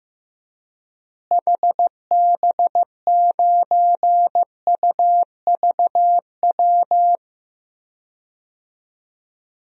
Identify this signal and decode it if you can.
Morse code, heard as a keyed tone, about 16 wpm, 700 Hz. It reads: HB9UVW
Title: DXCC-100 Call Signs - 6 Characters - Repeated 2x - 15wpm